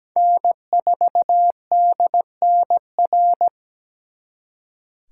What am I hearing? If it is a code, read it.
Morse code, heard as a keyed tone, about 20 wpm, 700 Hz. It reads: N4DNR
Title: Speed Racing - USA 1x3 Call Signs 17wpm